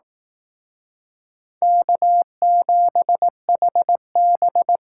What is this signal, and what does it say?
Morse code, heard as a keyed tone, about 18 wpm, 700 Hz. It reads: K7HB
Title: Speed Racing - USA 1x2 Call Signs 15wpm